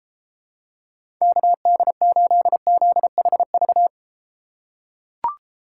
Morse code, heard as a keyed tone, about 34 wpm, 700 Hz. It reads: KD8ZHV